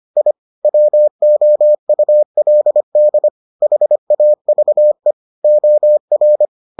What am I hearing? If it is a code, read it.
Morse code, heard as a keyed tone, about 25 wpm, 600 Hz. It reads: I WOULD HAVE OR